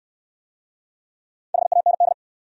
Morse code, heard as a keyed tone, about 68 wpm, 700 Hz. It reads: HAND